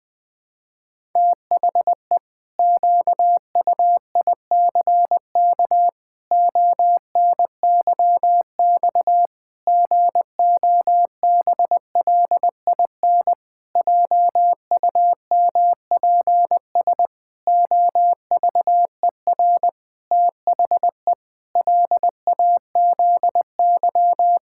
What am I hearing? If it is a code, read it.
Morse code, heard as a keyed tone, about 20 wpm, 700 Hz. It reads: THE QUICK ONYX GOBLIN JUMPS OVER THE LAZY
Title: Pangram Challenge v2 - 20wpm